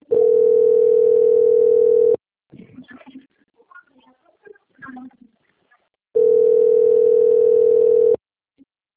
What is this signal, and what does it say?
Ringback tone (North America), 440 + 480 Hz.
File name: out- - - - - .7578